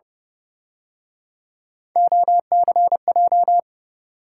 Morse code, heard as a keyed tone, about 30 wpm, 700 Hz. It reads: OCJ